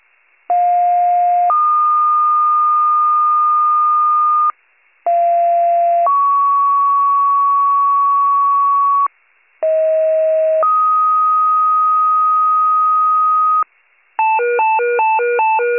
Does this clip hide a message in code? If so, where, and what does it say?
Yes: two-tone page from 0.5 to 13.6 s, heard as a one-second tone, then a three-second tone, about 710 then 1220 Hz, about 690 then 1120 Hz, about 640 then 1250 Hz.